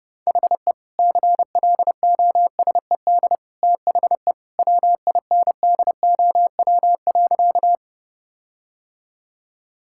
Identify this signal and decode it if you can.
Morse code, heard as a keyed tone, about 31 wpm, 700 Hz. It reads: HE CLOSED THE WINDOW.
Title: Sentences from Top 1600 Words 30wpm